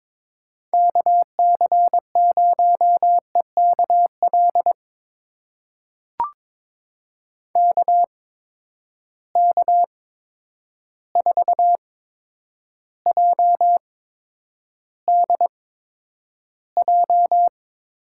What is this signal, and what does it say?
Morse code, heard as a keyed tone, about 22 wpm, spaced as for 9 wpm, 700 Hz. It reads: KC0EKL KK4JDJ
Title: USA Call Signs - 6 Characters - 8x Character Spacing - 22wpm